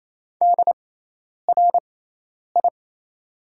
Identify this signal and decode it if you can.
Morse code, heard as a keyed tone, about 27 wpm, spaced as for 10 wpm, 690 Hz. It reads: DRI
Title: 6x Character Spacing - Top 500 Words - 28wpm